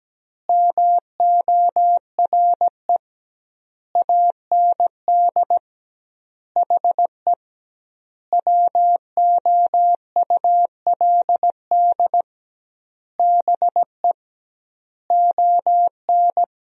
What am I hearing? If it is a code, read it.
Morse code, heard as a keyed tone, about 17 wpm, 700 Hz. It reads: MORE AND HE WOULD BE ON
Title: 2x Word-Spacing - Sentences from Top 300 Words - 17wpm